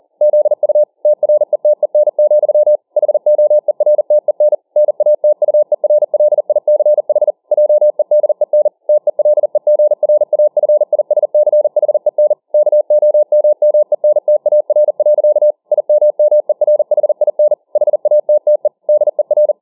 Morse code, heard as a keyed tone, about 40 wpm, 600 Hz. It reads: ZU TRETEN, HOERTEN NATUERLICH JEDEN TELEGRAFISCHEN KOMMENTAR. IMMERHIN HATTE DER